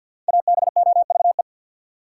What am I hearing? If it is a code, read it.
Morse code, heard as a keyed tone, about 46 wpm, 700 Hz. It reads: ABOVE
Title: Sentences from Top 1000 Words - Review Part 2 - New Word Drill - 50wpm